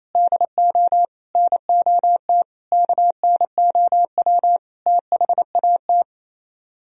Morse code, heard as a keyed tone, about 38 wpm, 700 Hz. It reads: DO NOT KNOW THAT